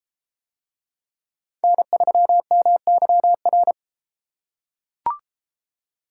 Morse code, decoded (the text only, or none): N3MYR